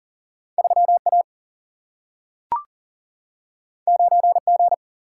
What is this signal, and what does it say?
Morse code, heard as a keyed tone, about 40 wpm, 700 Hz. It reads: 3 A 9 G
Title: Single-Letter Single-Number 40wpm